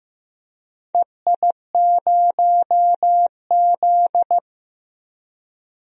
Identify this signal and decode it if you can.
Morse code, heard as a keyed tone, about 15 wpm, 700 Hz. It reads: EI0Z
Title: DXCC-100 Call Signs - 4 Characters - Repeated 2x - 15wpm